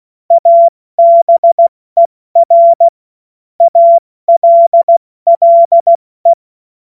Morse code, heard as a keyed tone, about 16 wpm, 680 Hz. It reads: ABER ALLE
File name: Morsekurs / Witze / Witz 04. Mai 2021, 70 F80